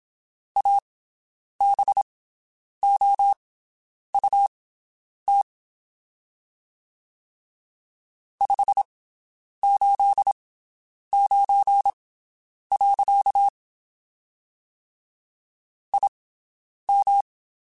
Morse code, heard as a keyed tone, about 27 wpm, spaced as for 9 wpm, 800 Hz. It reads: ABOUT 589. IM
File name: QSOs Spaced